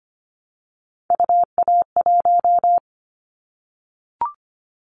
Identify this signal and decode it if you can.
Morse code, heard as a keyed tone, about 25 wpm, 700 Hz. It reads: UA1
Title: Callsign Prefixes for Top 50 DXCC - 25wpm